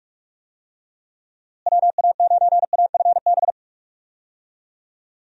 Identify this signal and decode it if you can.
Morse code, heard as a keyed tone, about 44 wpm, 700 Hz. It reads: WA9AFB
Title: USA Call Signs - 6 Characters - Repeated 3x - 45wpm